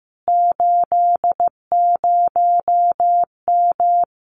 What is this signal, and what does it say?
Morse code, heard as a keyed tone, about 15 wpm, 700 Hz. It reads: 80M
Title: Common QSO Phrases Encore 15wpm